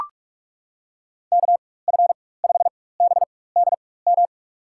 Morse code, heard as a keyed tone, about 42 wpm, spaced as for 20 wpm, 700 Hz. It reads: KF5BDM
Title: USA Call Signs - 6 Characters - 4x Character Spacing - 45wpm